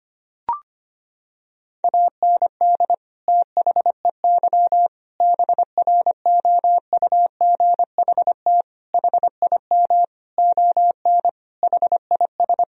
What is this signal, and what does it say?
Morse code, heard as a keyed tone, about 25 wpm, 700 Hz. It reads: AND THEY BROUGHT HIM ON HIS
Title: Sentences from Top 1000 Words - Review Part 2 - 25wpm